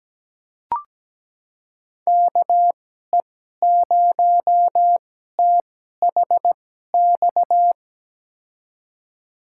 Morse code, heard as a keyed tone, about 17 wpm, spaced as for 12 wpm, 700 Hz. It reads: KE0THX
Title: USA Call Signs - 6 Characters - 2x Character Spacing - 17wpm